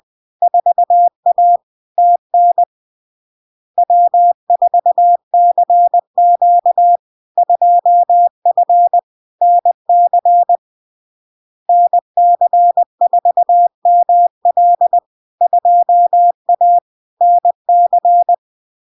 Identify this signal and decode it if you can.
Morse code, heard as a keyed tone, about 20 wpm, 700 Hz. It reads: 4A TN W4CQ 2F NC NC4ML 2A NC
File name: LARK-FD-2022-Log-in-CW-20wpm-700-Hz